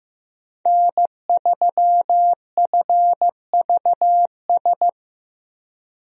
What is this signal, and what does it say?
Morse code, heard as a keyed tone, about 15 wpm, 700 Hz. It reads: N3FVS
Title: USA Call Signs - 5 Characters - Repeated 2x - 15wpm